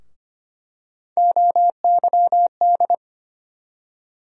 Morse code, heard as a keyed tone, about 25 wpm, 700 Hz. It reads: OYD
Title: USA Call Sign Postfixes - 3 Letters 25wpm